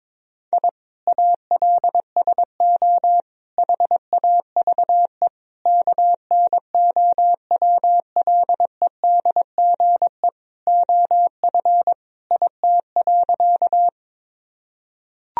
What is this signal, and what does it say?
Morse code, heard as a keyed tone, about 22 wpm, 700 Hz. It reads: I ALSO HAVE KNOWLEDGE OF IT.
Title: Sentences from Top 1000 Words - Review Part 3 - 22wpm